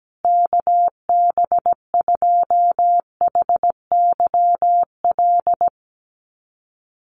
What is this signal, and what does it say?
Morse code, heard as a keyed tone, about 17 wpm, 700 Hz. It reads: KB2HYL
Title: USA Call Signs Repeated 2x 17wpm